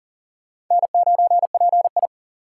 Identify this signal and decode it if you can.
Morse code, heard as a keyed tone, about 40 wpm, 700 Hz. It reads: N9PI
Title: USA Call Signs - 3x Character Spacing - 40wpm